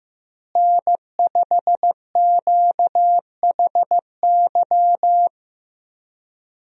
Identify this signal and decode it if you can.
Morse code, heard as a keyed tone, about 15 wpm, 700 Hz. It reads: N5QHY
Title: USA Call Signs - 5 Characters - Repeated 3x - 15wpm